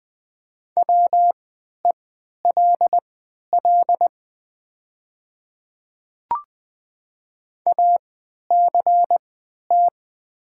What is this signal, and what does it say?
Morse code, heard as a keyed tone, about 20 wpm, spaced as for 11 wpm, 700 Hz. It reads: WELL ACT
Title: 3x Character Spacing - Top 200 Words - 20wpm